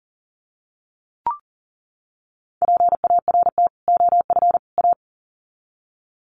Morse code, heard as a keyed tone, about 40 wpm, 700 Hz. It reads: PART OF A
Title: Sets of 3 Words 40wpm